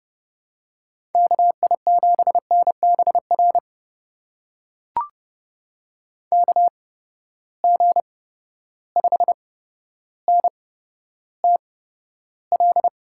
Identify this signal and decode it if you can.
Morse code, heard as a keyed tone, about 30 wpm, spaced as for 13 wpm, 700 Hz. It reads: KI7NBR KG5NTL
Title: USA Call Signs - 8x Character Spacing - 30wpm